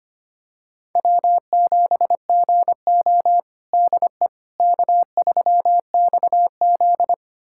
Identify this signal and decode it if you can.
Morse code, heard as a keyed tone, about 25 wpm, 700 Hz. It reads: W7GO DE K3XZ
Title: USA Callsign DE Callsign - 4 Characters - 25wpm